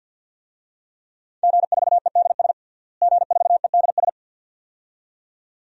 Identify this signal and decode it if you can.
Morse code, heard as a keyed tone, about 49 wpm, 700 Hz. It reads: G4EDS G4EDS
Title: DXCC-100 Call Signs - 5 Characters - Repeated 2x - 50wpm